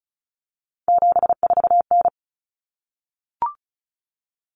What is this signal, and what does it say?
Morse code, heard as a keyed tone, about 34 wpm, 700 Hz. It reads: 74N
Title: Common Cut Numbers - 3-Digit 35wpm